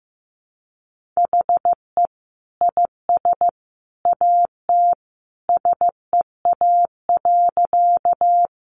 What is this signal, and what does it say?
Morse code, heard as a keyed tone, about 15 wpm, 700 Hz. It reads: HE IS AT SEA.
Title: Sentences from Top 300 Words 15wpm